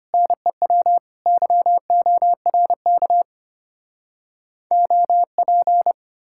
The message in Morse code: NEW YORK OP

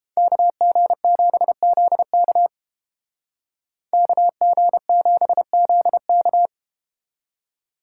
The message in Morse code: KG7ZK KG7ZK

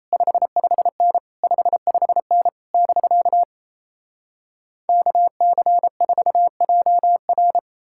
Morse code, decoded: 55N 55N <BK> KC4JR

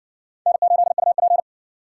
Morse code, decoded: N8UP

